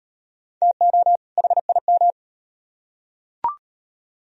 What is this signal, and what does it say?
Morse code, heard as a keyed tone, about 38 wpm, 700 Hz. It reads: TO HIM